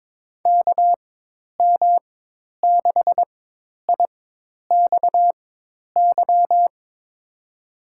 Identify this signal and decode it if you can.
Morse code, heard as a keyed tone, about 22 wpm, spaced as for 10 wpm, 700 Hz. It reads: KM6IXY
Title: USA Call Signs - 6 Characters - 4x Character Spacing - 22wpm